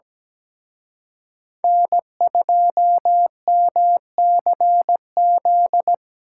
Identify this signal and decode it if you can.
Morse code, heard as a keyed tone, about 20 wpm, 700 Hz. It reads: N2MCZ
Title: Speed Racing - USA 1x3 Call Signs 17wpm